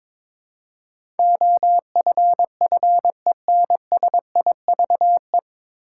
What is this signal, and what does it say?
Morse code, heard as a keyed tone, about 22 wpm, 700 Hz. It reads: OFFENSIVE
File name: 9 Letter Words - First 3 Letters Given - 22wpm